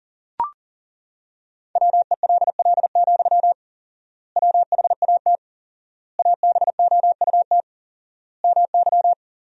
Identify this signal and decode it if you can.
Morse code, heard as a keyed tone, about 40 wpm, 700 Hz. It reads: WELL, WHAT ABOUT MY
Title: 4x Word-Spacing - Sentences from Top 500 Words - 40wpm